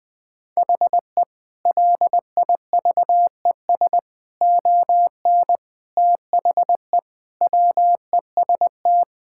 Morse code, heard as a keyed tone, about 20 wpm, 700 Hz. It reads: HE LIVES ON THE WEST
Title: Sentences with 6-Word Phrases - 20wpm